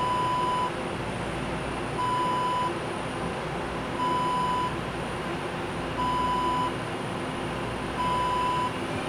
There is no code.